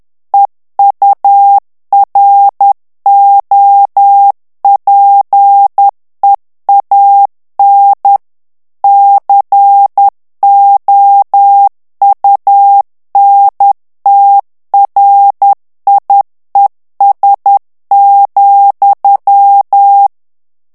Morse code, heard as a keyed tone, about 10 wpm, 800 Hz. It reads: EUROPEAN COUNTRIES,
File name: morse mp3 / ANGLAIS texte clair 2 vitesse 10 mots minute